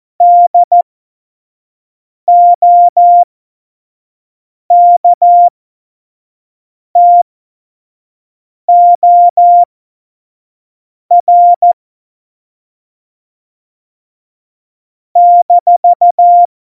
Morse code, heard as a keyed tone, about 14 wpm, spaced as for 5 wpm, 700 Hz. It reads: DOKTOR -
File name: Morsekurs / Witze / Witz 05. Dezember 2017, 25 F70